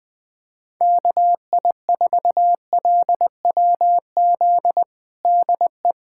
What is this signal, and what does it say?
Morse code, heard as a keyed tone, about 20 wpm, 700 Hz. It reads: KI4LWZ DE
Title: USA Callsign DE Callsign - 6 Characters - 20wpm